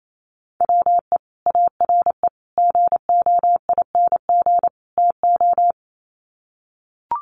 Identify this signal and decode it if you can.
Morse code, heard as a keyed tone, about 28 wpm, 700 Hz. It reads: WE ARE GOING TO
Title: Sets of 4 Words 28wpm